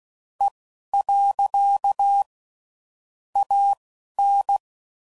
Morse code, heard as a keyed tone, about 16 wpm, spaced as for 11 wpm, 800 Hz. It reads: E. AN
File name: CW Code Groups @10WPM / Technical Subjects / XC